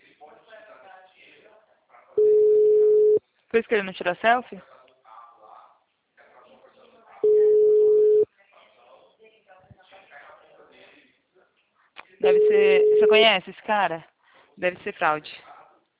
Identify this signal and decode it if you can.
Ringback tone (Europe, 425 Hz), with speech.